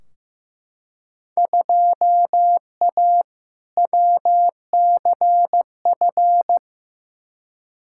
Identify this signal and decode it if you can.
Morse code, heard as a keyed tone, about 15 wpm, 700 Hz. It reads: 2A WCF
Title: Field Day Exchanges Encore - 15wpm